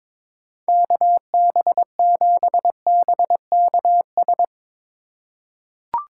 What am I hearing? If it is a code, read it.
Morse code, heard as a keyed tone, about 22 wpm, 700 Hz. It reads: KB7BKS
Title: USA Call Signs - 3x Character Spacing - 22wpm